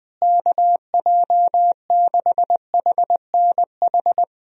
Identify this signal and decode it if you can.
Morse code, heard as a keyed tone, about 20 wpm, 700 Hz. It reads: KJ6HNH
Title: USA Call Signs - 6 Characters - Repeated 3x - 20wpm